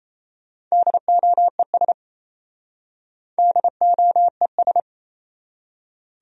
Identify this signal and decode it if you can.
Morse code, heard as a keyed tone, about 38 wpm, 700 Hz. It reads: DOES DOES